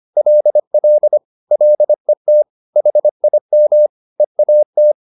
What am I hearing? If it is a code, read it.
Morse code, heard as a keyed tone, about 25 wpm, 600 Hz. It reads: LL LET HIM EAT
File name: the old man and the sea part 1 25wpm